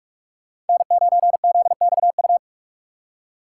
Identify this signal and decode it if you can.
Morse code, heard as a keyed tone, about 46 wpm, 700 Hz. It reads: N9ZXU
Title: USA Call Signs - 3x Character Spacing - 45wpm